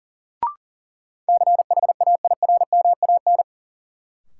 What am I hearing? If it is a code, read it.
Morse code, heard as a keyed tone, about 40 wpm, 700 Hz. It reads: CHAIRMAN
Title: Sets of 3 Words - New Word Drill - 40wpm